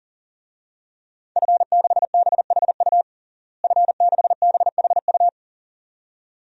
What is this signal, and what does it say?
Morse code, heard as a keyed tone, about 37 wpm, 700 Hz. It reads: F6BHU F6BHU
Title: DXCC-100 Call Signs - 5 Characters - Repeated 2x - 40wpm